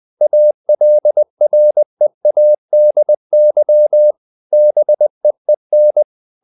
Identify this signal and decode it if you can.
Morse code, heard as a keyed tone, about 20 wpm, 600 Hz. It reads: ALREADY BEEN
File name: the old man and the sea part 2 20wpm